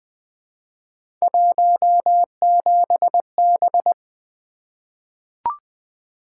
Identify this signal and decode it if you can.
Morse code, heard as a keyed tone, about 20 wpm, 700 Hz. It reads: 17B
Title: Three Letter-Number Combinations - 20wpm